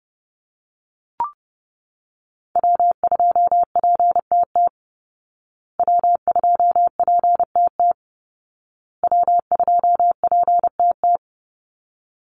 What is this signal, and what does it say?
Morse code, heard as a keyed tone, about 30 wpm, 700 Hz. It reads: W2PTT W2PTT W2PTT
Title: Call Signs Repeated 2x - 30wpm